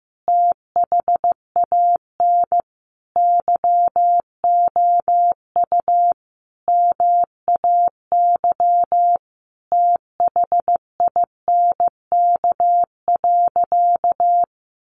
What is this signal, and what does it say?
Morse code, heard as a keyed tone, about 15 wpm, 700 Hz. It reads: THAN YOU MAY THINK.
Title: Sentences from Top 1100 Words 15wpm